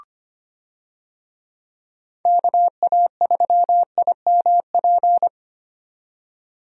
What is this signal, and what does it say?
Morse code, heard as a keyed tone, about 25 wpm, 700 Hz. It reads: KA3IMP
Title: USA Call Signs - 6 Characters - 25wpm